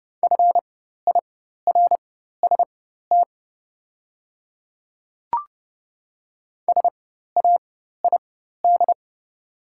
Morse code, heard as a keyed tone, about 30 wpm, spaced as for 14 wpm, 700 Hz. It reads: FIRST SAID
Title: 4x Character Spacing - Top 100 Words - 30wpm